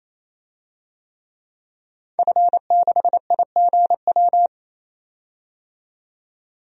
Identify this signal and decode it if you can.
Morse code, heard as a keyed tone, about 28 wpm, 700 Hz. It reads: F6IGW
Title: DXCC-100 Call Signs - 5 Characters - 28wpm